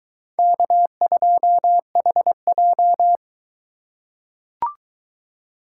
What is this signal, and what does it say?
Morse code, heard as a keyed tone, about 21 wpm, 700 Hz. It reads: K2HJ